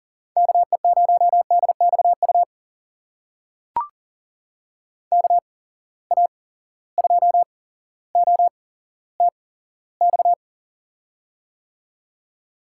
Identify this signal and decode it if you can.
Morse code, heard as a keyed tone, about 40 wpm, spaced as for 17 wpm, 700 Hz. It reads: KE0DXU KA2OTX
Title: USA Call Signs - 6 Characters - 8x Character Spacing - 40wpm